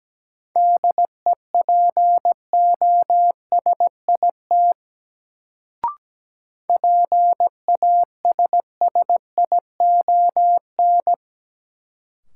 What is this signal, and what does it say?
Morse code, heard as a keyed tone, about 17 wpm, 700 Hz. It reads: DEPOSIT PASSION
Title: Sentences from Top 1300 Words - New Word Drill Alt - 17wpm